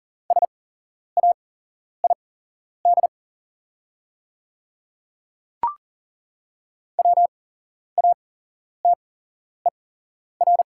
Morse code, heard as a keyed tone, about 40 wpm, spaced as for 11 wpm, 700 Hz. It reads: SAID WATER
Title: 8x Character Spacing - Top 100 Words - 40wpm